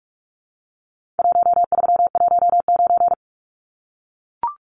Morse code, heard as a keyed tone, about 44 wpm, 700 Hz. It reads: 1319
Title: Numbers - Contest Style - 4-Digit 45wpm